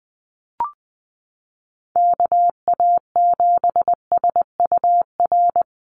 Morse code, heard as a keyed tone, about 20 wpm, 700 Hz. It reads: KA7SUR
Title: USA Call Signs Repeated 2x 20wpm